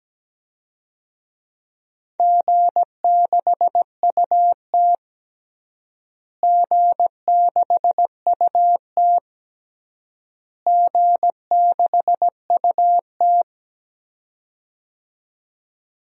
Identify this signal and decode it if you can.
Morse code, heard as a keyed tone, about 17 wpm, 700 Hz. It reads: G6UT G6UT G6UT
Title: DXCC-100 Call Signs - 4 Characters - Repeated 3x - 17wpm